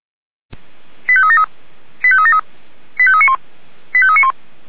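Five-tone selective call (ZVEI), heard as quick runs of five tones, tones of about 70 ms. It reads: 85373 85373 85391 85391
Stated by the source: ZVEI five-tone call 85381, received on SDR.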